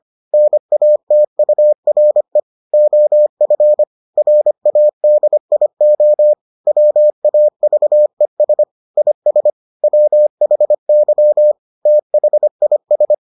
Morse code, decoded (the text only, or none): NATURE OF RADIO WAVES IS WHY THIS